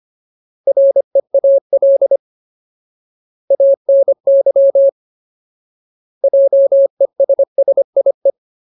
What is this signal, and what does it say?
Morse code, heard as a keyed tone, about 25 wpm, 550 Hz. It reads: REAL ANY JESSIE
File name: Ultimate Word List / 25WPM Ultimate